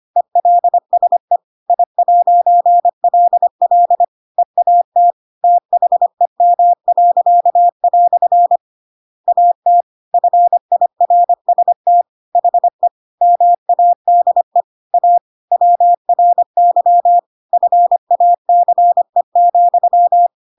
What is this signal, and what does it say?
Morse code, heard as a keyed tone, about 25 wpm, 700 Hz. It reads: ELSE I'LL EAT THEM." AT FIRST HE MADE A WRY FACE,
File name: Pinocchio / Pinocchio ch7 25 25 ik1hge